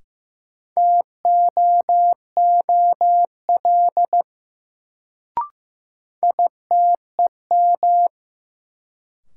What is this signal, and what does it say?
Morse code, heard as a keyed tone, about 15 wpm, 700 Hz. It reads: TOOL ITEM
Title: Sentences from Top 1000 Words - Review Part 6 - New Word Drill - 15wpm